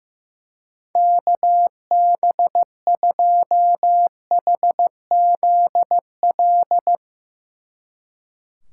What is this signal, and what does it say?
Morse code, heard as a keyed tone, about 15 wpm, 700 Hz. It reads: KB2HZL
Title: Speed Racing - USA Call Signs 15wpm